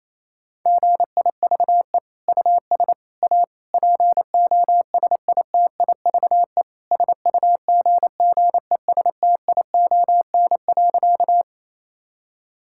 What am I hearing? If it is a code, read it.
Morse code, heard as a keyed tone, about 27 wpm, 700 Hz. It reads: GIVE US A POSITIVE SUGGESTION.
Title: Sentences from Top 1600 Words 28wpm